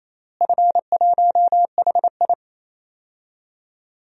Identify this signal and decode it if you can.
Morse code, heard as a keyed tone, about 28 wpm, 700 Hz. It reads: F1HI
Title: DXCC-100 Call Signs - 4 Characters - Repeated 3x - 28wpm